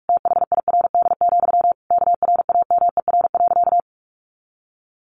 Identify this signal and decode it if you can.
Morse code, heard as a keyed tone, about 44 wpm, 700 Hz. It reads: THIRD, KRAMER.
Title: A Sound Of Thunder 45wpm